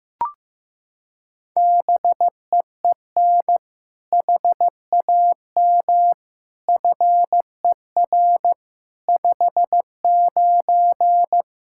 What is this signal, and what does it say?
Morse code, heard as a keyed tone, about 15 wpm, 700 Hz. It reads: BEEN HAM FER 59
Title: Practice QSO - Exchange 3 - More Info - 15wpm